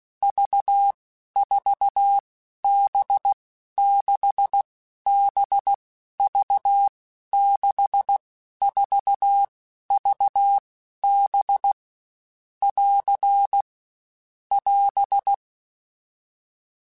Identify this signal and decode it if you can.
Morse code, heard as a keyed tone, about 16 wpm, 800 Hz. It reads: V 4 B 6 B V 6 4 V B <AR> <AS>
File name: Basic Letter Groups By7Groups / VB46 / VB46@15